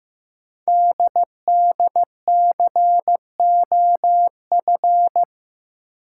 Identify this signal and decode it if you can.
Morse code, heard as a keyed tone, about 15 wpm, 700 Hz. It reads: DDCOF
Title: Code Groups - 5 Letters - 15wpm